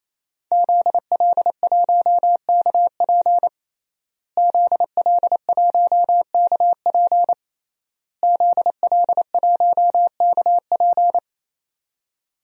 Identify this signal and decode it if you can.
Morse code, heard as a keyed tone, about 28 wpm, 700 Hz. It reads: ZL1KP ZL1KP ZL1KP
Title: DXCC-100 Call Signs - 5 Characters - Repeated 3x - 28wpm